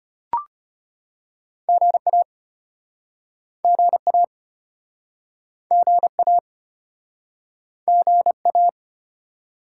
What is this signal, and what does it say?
Morse code, heard as a keyed tone, about 32 wpm, 700 Hz. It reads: GA GA GA GA